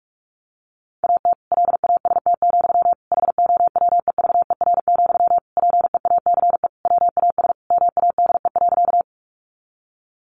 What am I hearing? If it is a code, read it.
Morse code, heard as a keyed tone, about 44 wpm, 700 Hz. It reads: AT LAST, HOWEVER, PEACE WAS MADE.
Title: Sentences from Top 1500 Words 45wpm